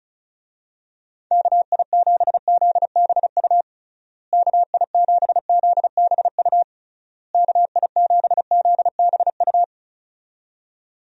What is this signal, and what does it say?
Morse code, heard as a keyed tone, about 34 wpm, 700 Hz. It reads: KI7ZBU KI7ZBU KI7ZBU
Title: USA Call Signs - 6 Characters - Repeated 3x - 35wpm